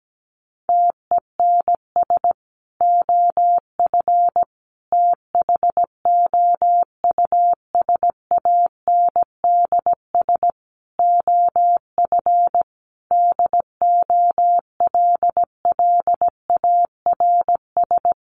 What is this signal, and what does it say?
Morse code, heard as a keyed tone, about 17 wpm, 700 Hz. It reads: TENS OF THOUSANDS OF DOLLARS
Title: Sets of 5 Words Encore2 17wpm